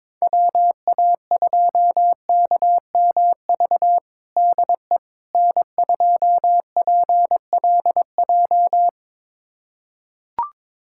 Morse code, heard as a keyed tone, about 22 wpm, 700 Hz. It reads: WA2KMV DE N2PLJ
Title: USA Callsign DE Callsign 22wpm